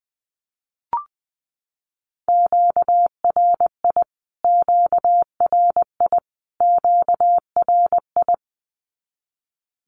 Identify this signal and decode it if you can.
Morse code, heard as a keyed tone, about 20 wpm, 700 Hz. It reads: QRI QRI QRI
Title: Q-Codes Repeated 3x 20wpm